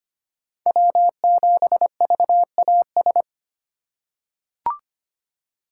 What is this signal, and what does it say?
Morse code, heard as a keyed tone, about 25 wpm, 700 Hz. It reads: W7VAS
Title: K1USN SST QSO Practice - 25wpm